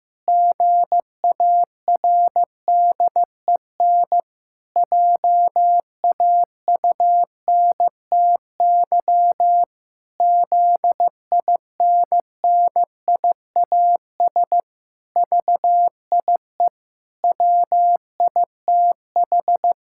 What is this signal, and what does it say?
Morse code, heard as a keyed tone, about 15 wpm, 700 Hz. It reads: GARDEN JAUNTY ZINNIAS VIE WITH
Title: Pangram Challenge v2 - 15wpm